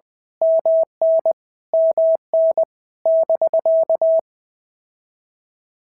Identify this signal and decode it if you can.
Morse code, heard as a keyed tone, about 20 wpm, 650 Hz. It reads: MN MN <BK>
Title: POTA QSO Practice - 20wpm